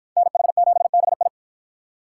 Morse code, heard as a keyed tone, about 53 wpm, 700 Hz. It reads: NH7BI